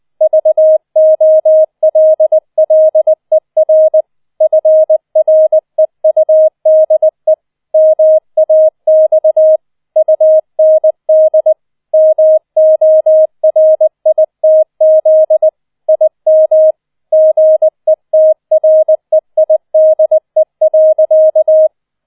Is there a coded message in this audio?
Yes: Morse code from 0.2 to 21.7 s, heard as a keyed tone, about 20 wpm, 630 Hz. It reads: VOLLER FREUDE MAX UND MORITZ IM GETREIDE.